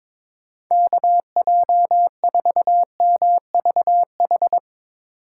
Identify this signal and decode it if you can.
Morse code, heard as a keyed tone, about 24 wpm, 700 Hz. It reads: KJ4MVH
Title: Speed Racing - USA Call Signs 22wpm